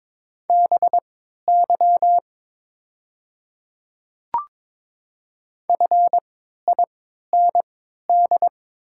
Morse code, heard as a keyed tone, about 22 wpm, spaced as for 12 wpm, 690 Hz. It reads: BY FIND
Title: 3x Character Spacing - Top 300 Words - 22wpm